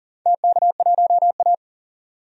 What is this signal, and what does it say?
Morse code, heard as a keyed tone, about 40 wpm, 700 Hz. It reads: TK1A
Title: DXCC-100 Call Signs - 4 Characters - 40wpm